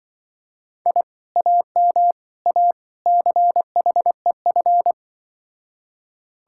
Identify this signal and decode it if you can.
Morse code, heard as a keyed tone, about 24 wpm, 700 Hz. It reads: I AM A CHEF
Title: Speed-Racing Common QSO Phrases - 20wpm